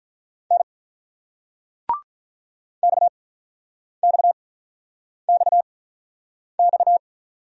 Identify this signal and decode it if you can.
Morse code, heard as a keyed tone, about 47 wpm, spaced as for 9 wpm, 700 Hz. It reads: N XXXX